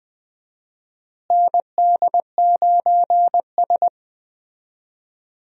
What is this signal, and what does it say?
Morse code, heard as a keyed tone, about 20 wpm, 700 Hz. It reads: ND9S
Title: USA Call Signs - 4 Characters - Repeated 3x - 20wpm